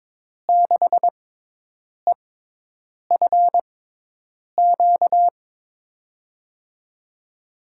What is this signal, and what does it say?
Morse code, heard as a keyed tone, about 22 wpm, spaced as for 8 wpm, 700 Hz. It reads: 6EFQ
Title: USA Call Signs - 6 Characters - 6x Character Spacing - 22wpm